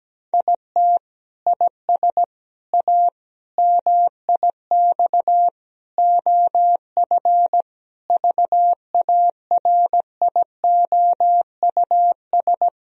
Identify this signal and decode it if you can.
Morse code, heard as a keyed tone, about 17 wpm, 700 Hz. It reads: IT IS A MIX OF VARIOUS
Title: Sentences from Top 1000 Words - Review Part 4 - 17wpm